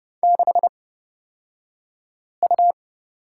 Morse code, heard as a keyed tone, about 30 wpm, spaced as for 5 wpm, 700 Hz. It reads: 6U
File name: Lesson 36d - BT - Two Characters 30wpm